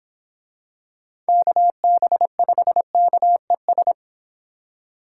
Morse code, heard as a keyed tone, about 26 wpm, 700 Hz. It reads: KB5KES